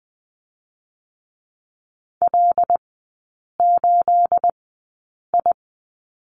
Morse code, heard as a keyed tone, about 20 wpm, spaced as for 8 wpm, 690 Hz. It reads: L8I
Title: 212 Lesson 34e - Number 8 - Three Characters 20wpm